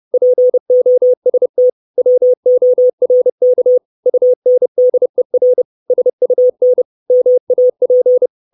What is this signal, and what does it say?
Morse code, heard as a keyed tone, about 30 wpm, 500 Hz. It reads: POST WORK UNDER SUN MAP
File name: 632 Words 30WPM